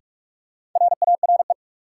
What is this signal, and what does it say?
Morse code, heard as a keyed tone, about 42 wpm, 700 Hz. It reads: RARE
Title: Sentences from Top 1100 Words - New Word Drill - 45wpm